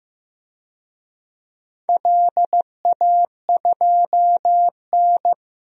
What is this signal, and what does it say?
Morse code, heard as a keyed tone, about 15 wpm, 700 Hz. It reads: LA2N
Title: DXCC-100 Call Signs - 4 Characters - Repeated 2x - 15wpm